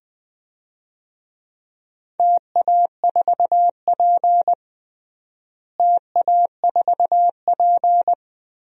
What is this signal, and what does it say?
Morse code, heard as a keyed tone, about 20 wpm, 700 Hz. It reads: TA4P TA4P
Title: DXCC-100 Call Signs - 4 Characters - Repeated 3x - 20wpm